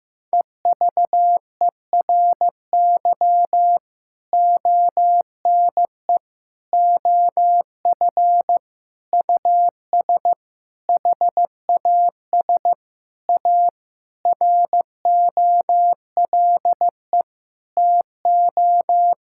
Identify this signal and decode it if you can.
Morse code, heard as a keyed tone, about 15 wpm, 700 Hz. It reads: EVERY ONE OF US HAS A ROLE TO
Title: Sentences from Top 1000 Words - Review Part 6 - 15wpm